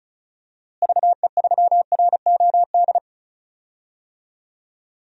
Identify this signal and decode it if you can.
Morse code, heard as a keyed tone, about 36 wpm, 700 Hz. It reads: VE3ROD
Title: DXCC-100 Call Signs - 6 Characters - Repeated 3x - 35wpm